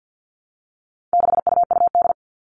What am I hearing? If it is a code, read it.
Morse code, heard as a keyed tone, about 50 wpm, 700 Hz. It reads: 6UUD
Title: All Cut Numbers - 4-Digit 50wpm